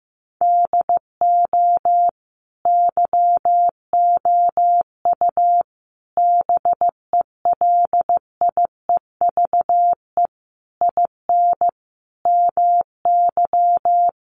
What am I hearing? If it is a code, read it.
Morse code, heard as a keyed tone, about 15 wpm, 700 Hz. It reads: DO YOU BELIEVE IN MY
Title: Sentences from Top 1200 Words 15wpm